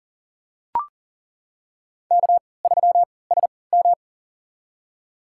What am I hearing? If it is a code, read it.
Morse code, heard as a keyed tone, about 40 wpm, spaced as for 23 wpm, 700 Hz. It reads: K3SM
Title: USA Call Signs - 4 Characters - 3x Character Spacing - 40wpm